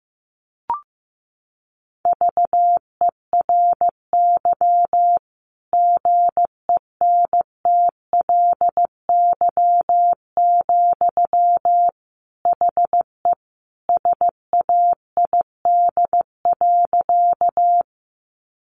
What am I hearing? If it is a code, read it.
Morse code, heard as a keyed tone, about 15 wpm, 700 Hz. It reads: VERY GENTLY, HE SAID.
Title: Sentences from Top 1500 Words 15wpm